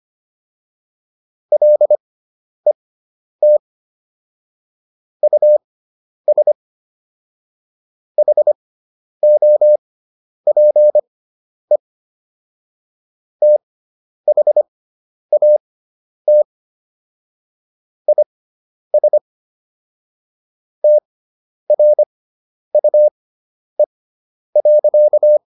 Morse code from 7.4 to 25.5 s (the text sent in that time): HOPE THAT IS TRUE.